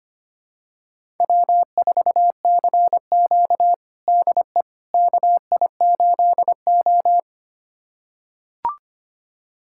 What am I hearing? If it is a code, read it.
Morse code, heard as a keyed tone, about 25 wpm, 700 Hz. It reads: W4CQ DE KI8O
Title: USA Callsign DE Callsign - 4 Characters - 25wpm